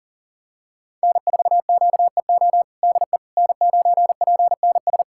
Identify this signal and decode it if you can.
Morse code, heard as a keyed tone, about 40 wpm, 700 Hz. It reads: N4QEO DE N9PNS
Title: USA Callsign DE Callsign - 5 Characters - 40wpm